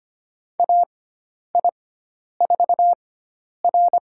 Morse code, heard as a keyed tone, about 25 wpm, spaced as for 10 wpm, 700 Hz. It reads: AI4R
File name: Lesson 16g - Letter c - Call Signs 25wpm